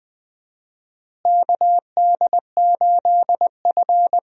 Morse code, heard as a keyed tone, about 21 wpm, 700 Hz. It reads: KD8F
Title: USA Call Signs - 4 Characters - Repeated 2x - 20wpm